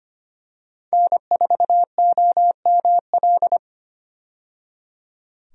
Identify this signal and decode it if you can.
Morse code, heard as a keyed tone, about 25 wpm, 700 Hz. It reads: N4OML
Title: USA Call Signs - 5 Characters - 25wpm